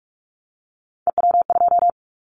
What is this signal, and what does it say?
Morse code, heard as a keyed tone, about 46 wpm, 700 Hz. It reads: EW2
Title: Callsign Prefixes for Top 100 DXCC - 45wpm